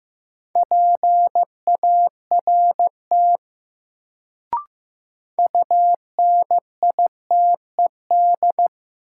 Morse code, heard as a keyed tone, about 15 wpm, 700 Hz. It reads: PART UNITED
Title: Sets of 2 Words - New Word Drill - 15wpm